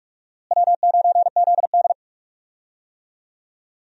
Morse code, heard as a keyed tone, about 46 wpm, 700 Hz. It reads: W9ZD
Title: USA Call Signs - 4 Characters - Repeated 2x - 45wpm